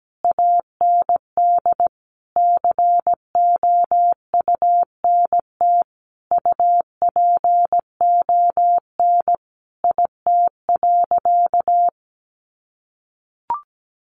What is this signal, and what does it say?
Morse code, heard as a keyed tone, about 17 wpm, 700 Hz. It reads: AND COUNT UPON IT.
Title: Sentences from Top 1500 Words 17wpm